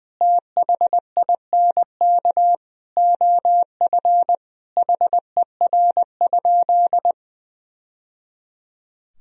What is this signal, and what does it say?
Morse code, heard as a keyed tone, about 21 wpm, 700 Hz. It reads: THINK OF HER?
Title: Sentences from Top 1000 Words - Review Part 4 - 20wpm